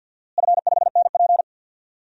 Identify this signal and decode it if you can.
Morse code, heard as a keyed tone, about 50 wpm, 700 Hz. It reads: F5NP
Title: DXCC-100 Call Signs - 4 Characters - Repeated 3x - 50wpm